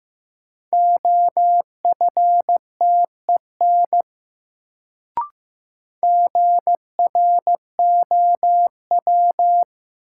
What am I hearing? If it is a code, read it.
Morse code, heard as a keyed tone, about 15 wpm, 700 Hz. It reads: OFTEN GROW
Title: Sentences from Top 300 Words - New Word Drill - 15wpm